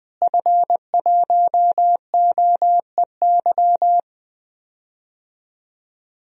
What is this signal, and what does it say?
Morse code, heard as a keyed tone, about 21 wpm, 700 Hz. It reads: F1OEY
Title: DXCC-100 Call Signs - 5 Characters - Repeated 3x - 20wpm